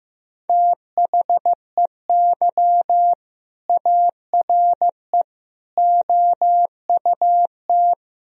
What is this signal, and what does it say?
Morse code, heard as a keyed tone, about 15 wpm, 700 Hz. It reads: THEY ARE OUT
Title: Sentences from Top 1000 Words - Review Part 1 - 15wpm